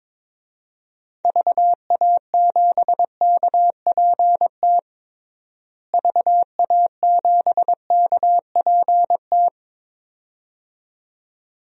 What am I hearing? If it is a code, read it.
Morse code, heard as a keyed tone, about 22 wpm, 700 Hz. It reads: VA7KPT VA7KPT
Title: DXCC-100 Call Signs - 6 Characters - Repeated 2x - 22wpm